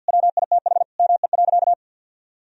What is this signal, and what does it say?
Morse code, heard as a keyed tone, about 51 wpm, 700 Hz. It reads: WITH ME.
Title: Sentences from Top 1000 Words - Review Part 3 - 50wpm